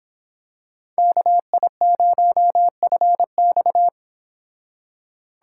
Morse code, heard as a keyed tone, about 26 wpm, 700 Hz. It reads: KI0FX